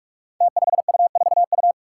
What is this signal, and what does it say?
Morse code, heard as a keyed tone, about 44 wpm, 700 Hz. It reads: T5U4U
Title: Five Challenging Letter-Number Combinations 45wpm